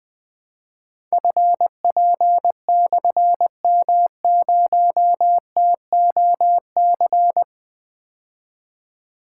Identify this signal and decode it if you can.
Morse code, heard as a keyed tone, about 20 wpm, 700 Hz. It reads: FP/M0TOC
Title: Difficult Call Signs 20wpm